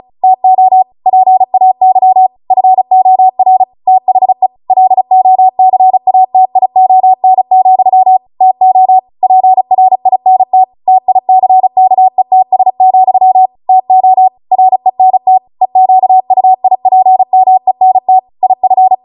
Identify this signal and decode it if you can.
Morse code, heard as a keyed tone, about 36 wpm, 750 Hz. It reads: TO PAY FOR THE LOCATION, TO PRINT TICKETS, TO RENT EQUIPMENT IF
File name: Morse / 35WPM